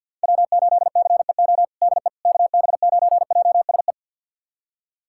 Morse code, heard as a keyed tone, about 50 wpm, 700 Hz. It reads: W8CEO DE KB9JSE
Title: USA Callsign DE Callsign 50wpm